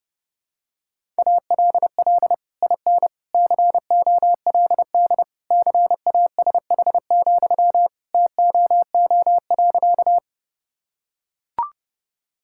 Morse code, heard as a keyed tone, about 32 wpm, 700 Hz. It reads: ALL IN COLD CASH, TOO.
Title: Sentences from Top 1000 Words - Review Part 5 - 30wpm